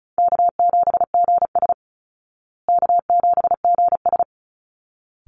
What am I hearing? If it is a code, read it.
Morse code, heard as a keyed tone, about 34 wpm, 700 Hz. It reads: K7GS K7GS
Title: Call Signs Repeated 3x - 35wpm